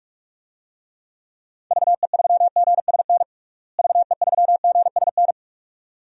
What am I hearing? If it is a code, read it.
Morse code, heard as a keyed tone, about 44 wpm, 700 Hz. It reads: VE3GSN VE3GSN
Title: DXCC-100 Call Signs - 6 Characters - Repeated 2x - 45wpm